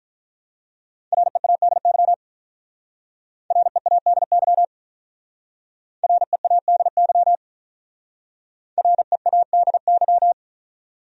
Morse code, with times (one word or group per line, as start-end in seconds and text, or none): 1.1-2.1 READY
3.5-4.6 READY
6.0-7.3 READY
8.8-10.3 READY